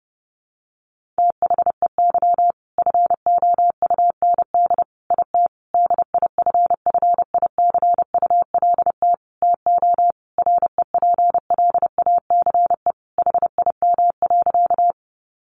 Morse code, heard as a keyed tone, about 30 wpm, 700 Hz. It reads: THEY FOUND IT DIFFICULT TO REPLACE HIM.
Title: Sentences from Top 1400 Words 30wpm